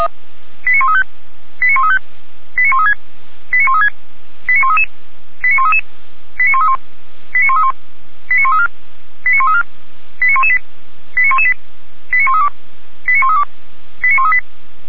Five-tone selective call (ZVEI), heard as quick runs of five tones, tones of about 70 ms. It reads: 89135 89135 89136 89136 89133 89133 89131 89131 89134 89134 89118 89118 89132 89132 89138